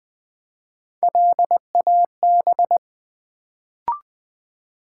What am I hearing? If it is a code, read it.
Morse code, heard as a keyed tone, about 20 wpm, 700 Hz. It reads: LAB
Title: Build A Word - 1 Letter At A Time - 3-Letter Words - 20wpm